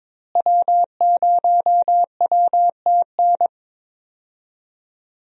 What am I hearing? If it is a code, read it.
Morse code, heard as a keyed tone, about 22 wpm, 700 Hz. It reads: W0WTN
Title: USA Call Signs - 5 Characters - Repeated 3x - 22wpm